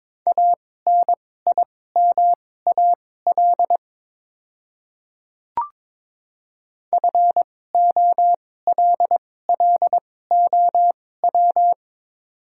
Morse code, heard as a keyed tone, about 22 wpm, spaced as for 16 wpm, 700 Hz. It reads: ANIMAL FOLLOW
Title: 2x Character Spacing - Top 300 Words - 22wpm